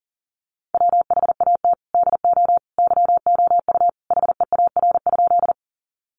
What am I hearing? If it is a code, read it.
Morse code, heard as a keyed tone, about 40 wpm, 700 Hz. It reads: WHAT DO YOU HEAR?
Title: Sentences from Top 300 Words 40wpm